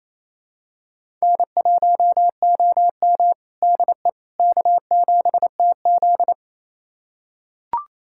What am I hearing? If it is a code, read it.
Morse code, heard as a keyed tone, about 28 wpm, 700 Hz. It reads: N1OM DE K7TZ
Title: USA Callsign DE Callsign - 4 Characters - 28wpm